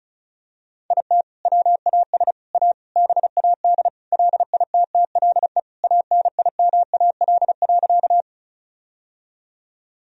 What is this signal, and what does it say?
Morse code, heard as a keyed tone, about 36 wpm, 700 Hz. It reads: IT WAS A BAD LITTLE ANIMAL.
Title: Sentences from Top 1000 Words - Review Part 1 - 35wpm